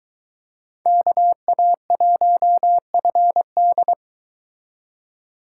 Morse code, heard as a keyed tone, about 21 wpm, 700 Hz. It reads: KA1FD
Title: Speed Racing - USA 2x2 Call Signs 17wpm